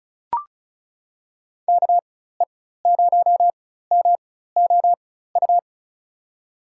Morse code, heard as a keyed tone, about 36 wpm, spaced as for 16 wpm, 700 Hz. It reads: KE0MOU
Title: USA Call Signs - 6 Characters - 4x Character Spacing - 35wpm